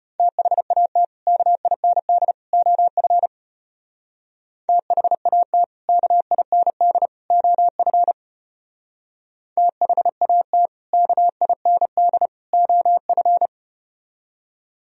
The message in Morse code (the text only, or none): THAT KIND OF THAT KIND OF THAT KIND OF